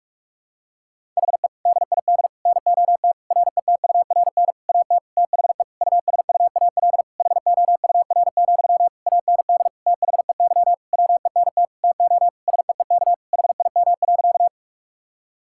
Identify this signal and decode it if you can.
Morse code, heard as a keyed tone, about 46 wpm, 700 Hz. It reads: HE DID NOT RETURN AT THE USUAL HOUR, AND THEY WENT TO SEEK HIM.
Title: Sentences from Top 1600 Words 45wpm